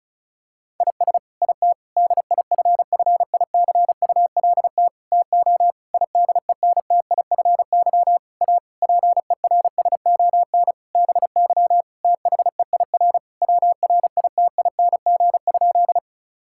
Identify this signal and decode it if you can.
Morse code, heard as a keyed tone, about 36 wpm, 700 Hz. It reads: IS IT DIFFICULT TO IDENTIFY A PERSON BY THEIR WRITING?
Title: Sentences from Top 1000 Words - Review Part 6 - 35wpm